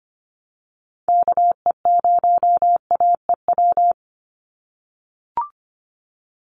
Morse code, heard as a keyed tone, about 25 wpm, 700 Hz. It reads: KE0AEW
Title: USA Call Signs 25wpm